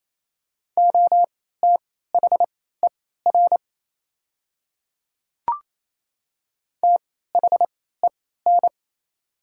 Morse code, heard as a keyed tone, about 28 wpm, spaced as for 16 wpm, 700 Hz. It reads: OTHER THEN
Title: 3x Character Spacing - Top 200 Words - 28wpm